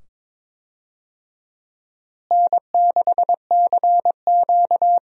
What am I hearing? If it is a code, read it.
Morse code, heard as a keyed tone, about 22 wpm, 700 Hz. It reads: N6CQ
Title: USA Call Signs - 4 Characters - Repeated 2x - 22wpm